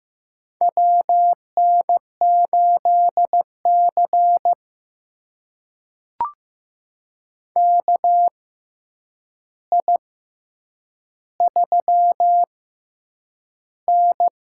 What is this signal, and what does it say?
Morse code, heard as a keyed tone, about 15 wpm, spaced as for 8 wpm, 700 Hz. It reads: WN8C KI3N
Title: USA Call Signs - 4 Characters - 6x Character Spacing - 15wpm